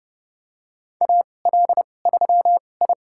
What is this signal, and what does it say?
Morse code, heard as a keyed tone, about 30 wpm, spaced as for 22 wpm, 700 Hz. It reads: AL3I
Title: USA Call Signs - 4 Characters - 2x Character Spacing - 30wpm